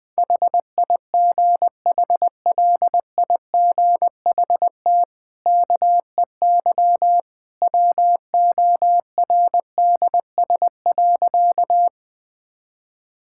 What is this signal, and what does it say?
Morse code, heard as a keyed tone, about 20 wpm, 700 Hz. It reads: HIGHLIGHT KEY WORDS.
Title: Easy Sentences with Hints - 20wpm